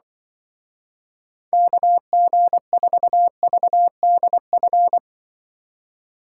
Morse code, heard as a keyed tone, about 24 wpm, 700 Hz. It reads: KG4VDF
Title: Speed Racing - USA 2x3 Call Signs 20wpm